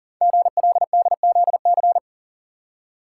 Morse code, heard as a keyed tone, about 43 wpm, 700 Hz. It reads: GLDZC
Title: Code Groups - 5 Letters - 40wpm